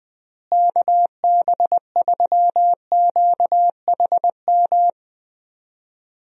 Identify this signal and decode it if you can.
Morse code, heard as a keyed tone, about 20 wpm, 700 Hz. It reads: KB3QHM
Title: USA Call Signs - 6 Characters - Repeated 3x - 20wpm